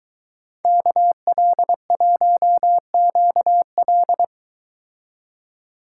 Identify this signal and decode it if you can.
Morse code, heard as a keyed tone, about 22 wpm, 700 Hz. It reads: KL1QL